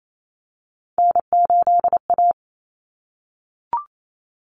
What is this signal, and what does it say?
Morse code, heard as a keyed tone, about 28 wpm, 700 Hz. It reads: N8A
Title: Common Cut Numbers - 3-Digit 28wpm